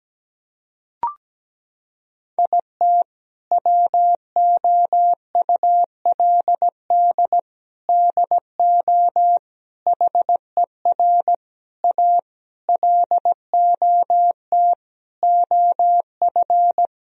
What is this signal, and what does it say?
Morse code, heard as a keyed tone, about 17 wpm, 700 Hz. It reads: IT WOULD DO HER A LOT OF
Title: Sentences from Top 1000 Words - Review Part 3 - 17wpm